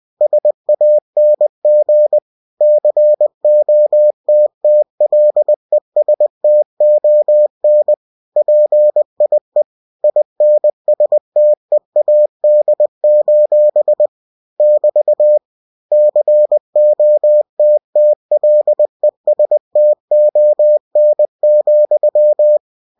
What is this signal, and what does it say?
Morse code, heard as a keyed tone, about 20 wpm, 600 Hz. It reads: SANG COTTLESTON PIE INSTEAD: <BT> COTTLESTON,
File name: cw-books / winnie-the-pooh / winnie-20wpm-page-0045